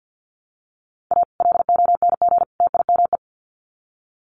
Morse code, heard as a keyed tone, about 50 wpm, 700 Hz. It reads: A LONG TIME
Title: Sets of 3 Words 50wpm